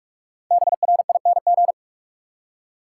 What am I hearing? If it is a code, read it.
Morse code, heard as a keyed tone, about 44 wpm, 700 Hz. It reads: BRING
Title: Five Letter Words - 45wpm